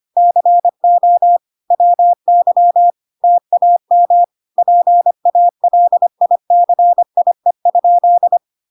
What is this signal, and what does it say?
Morse code, heard as a keyed tone, about 25 wpm, 700 Hz. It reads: CO WY TAM PALICIE?